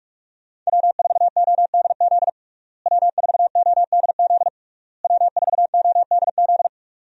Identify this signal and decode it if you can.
Morse code, heard as a keyed tone, about 46 wpm, 700 Hz. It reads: W4ODZ W4ODZ W4ODZ
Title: USA Call Signs - 5 Characters - Repeated 3x - 45wpm